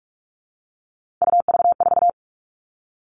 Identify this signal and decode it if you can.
Morse code, heard as a keyed tone, about 44 wpm, 700 Hz. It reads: UV4
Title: All Cut Numbers - 3-Digit 45wpm